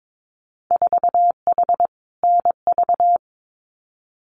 Morse code, heard as a keyed tone, about 22 wpm, 700 Hz. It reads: 4H NV
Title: Winter Field Day Exchanges Encore 22wpm